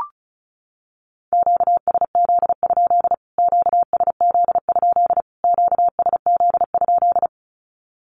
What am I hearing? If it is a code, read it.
Morse code, heard as a keyed tone, about 34 wpm, 700 Hz. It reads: QSZ? QSZ? QSZ?
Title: Q-Codes Repeated 3x 35wpm